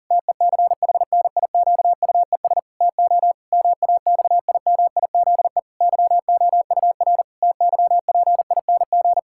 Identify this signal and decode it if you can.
Morse code, heard as a keyed tone, about 40 wpm, 700 Hz. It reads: TECHNIQUES TO MAXIMIZE YOUR TYPING
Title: Pangram Challenge v2 - 40wpm